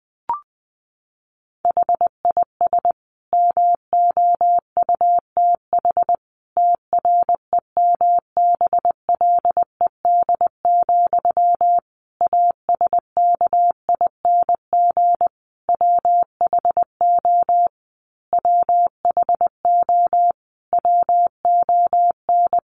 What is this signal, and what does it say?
Morse code, heard as a keyed tone, about 20 wpm, 700 Hz. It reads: HIS MOUTH TREMBLED, ASKING WHO WHO WON
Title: A Sound Of Thunder 20wpm